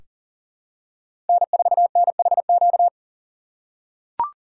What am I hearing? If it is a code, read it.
Morse code, heard as a keyed tone, about 40 wpm, 700 Hz. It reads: N4NHQ
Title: USA Call Signs - 6x Character Spacing - 40wpm